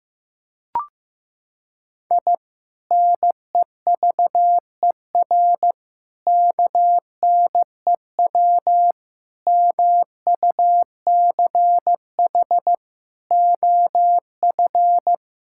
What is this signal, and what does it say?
Morse code, heard as a keyed tone, about 15 wpm, 700 Hz. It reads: I NEVER KNEW MUCH OF
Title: Sentences from Top 1000 Words - Review Part 6 - 15wpm